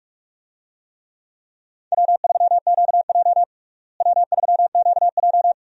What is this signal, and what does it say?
Morse code, heard as a keyed tone, about 46 wpm, 700 Hz. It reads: W3QJ W3QJ
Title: USA Call Signs - 4 Characters - Repeated 3x - 45wpm